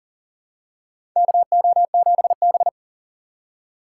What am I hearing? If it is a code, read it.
Morse code, heard as a keyed tone, about 40 wpm, 700 Hz. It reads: KO7B
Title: USA Call Signs - 2x Character Spacing - 40wpm